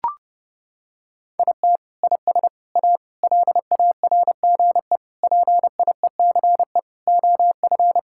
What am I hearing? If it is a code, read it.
Morse code, heard as a keyed tone, about 30 wpm, 700 Hz. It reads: IT IS A LARGE PIECE OF
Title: Sentences from Top 1000 Words - Review Part 5 - 30wpm